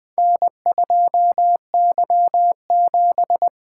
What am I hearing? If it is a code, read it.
Morse code, heard as a keyed tone, about 20 wpm, 700 Hz. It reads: N2Y7
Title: Four Letter-Number Combinations - 20wpm